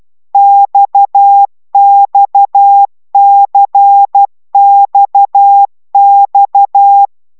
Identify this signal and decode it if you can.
Morse code, heard as a keyed tone, about 12 wpm, 800 Hz. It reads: XXCXX